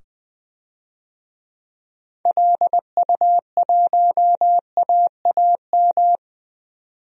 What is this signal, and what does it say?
Morse code, heard as a keyed tone, about 20 wpm, 700 Hz. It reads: LU1AAM
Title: DXCC-100 Call Signs - 6 Characters - 20wpm